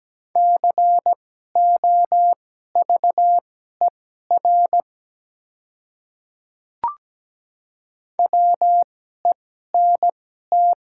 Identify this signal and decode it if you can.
Morse code, heard as a keyed tone, about 17 wpm, spaced as for 12 wpm, 700 Hz. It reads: COVER WENT
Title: 2x Character Spacing - Top 300 Words - 17wpm